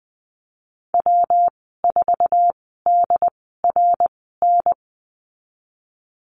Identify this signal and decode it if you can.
Morse code, heard as a keyed tone, about 20 wpm, spaced as for 15 wpm, 700 Hz. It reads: W4DRN
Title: 167 Lesson 27g - Number 7 - Call Signs 20wpm